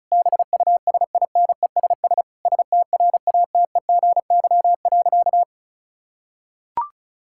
Morse code, heard as a keyed tone, about 34 wpm, 700 Hz. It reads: BUSINESS STRATEGY.
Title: Sentences from Top 1000 Words - Review Part 6 - 35wpm